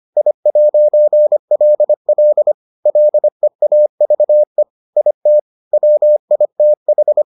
Morse code, heard as a keyed tone, about 25 wpm, 600 Hz. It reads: I'LL LEAVE IT WITH